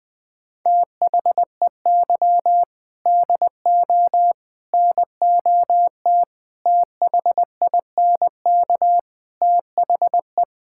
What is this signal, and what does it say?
Morse code, heard as a keyed tone, about 20 wpm, 700 Hz. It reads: THEY DO NOT THINK THE
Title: Sentences with 4-Word Phrases - 20wpm